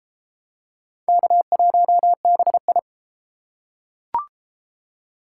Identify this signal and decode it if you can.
Morse code, heard as a keyed tone, about 32 wpm, 700 Hz. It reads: K1BI